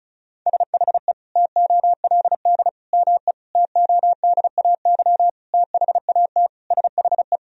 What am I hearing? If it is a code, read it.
Morse code, heard as a keyed tone, about 36 wpm, 700 Hz. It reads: SHE TOLD ME TODAY THAT SHE
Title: Sentences from Top 1000 Words - Review Part 4 - 35wpm